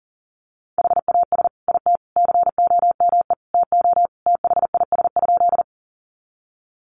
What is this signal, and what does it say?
Morse code, heard as a keyed tone, about 40 wpm, 700 Hz. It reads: HAS IT COME TO THIS?
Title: Sentences from Top 100 Words 40wpm